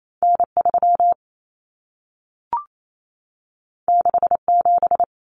Morse code, heard as a keyed tone, about 28 wpm, 700 Hz. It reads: N3 67
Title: Common Cut Numbers - 2-Digit 28wpm